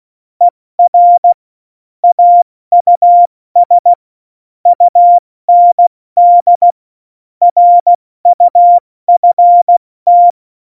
Morse code, heard as a keyed tone, about 16 wpm, spaced as for 14 wpm, 700 Hz. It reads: ER AUS UND RUFT
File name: Morsekurs / Witze / Witz 17. November 2020, 70 F80